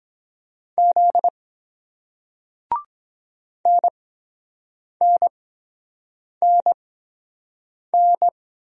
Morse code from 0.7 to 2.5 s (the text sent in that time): Z